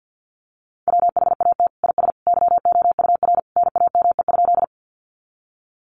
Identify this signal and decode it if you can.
Morse code, heard as a keyed tone, about 48 wpm, 700 Hz. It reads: WHAT IS YOUR NAME?
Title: Sentences from Top 200 Words 50wpm